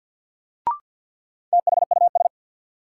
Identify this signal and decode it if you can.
Morse code, heard as a keyed tone, about 49 wpm, 700 Hz. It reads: THUS
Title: Sentences from Top 800 Words - New Word Drill - 50wpm